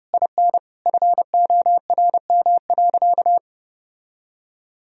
Morse code, heard as a keyed tone, about 30 wpm, 700 Hz. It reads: IN FORM.
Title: Sentences from Top 1000 Words - Review Part 1 - 30wpm